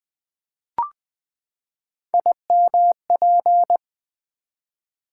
Morse code, heard as a keyed tone, about 20 wpm, 700 Hz. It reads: IMP
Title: Build A Word - 3 Letters At A Time - 6-Letter Words - 20wpm